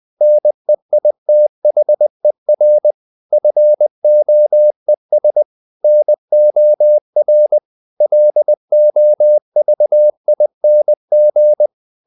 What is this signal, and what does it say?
Morse code, heard as a keyed tone, about 20 wpm, 600 Hz. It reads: NEITHER FOES NOR LOVING
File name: CW / Practice / If20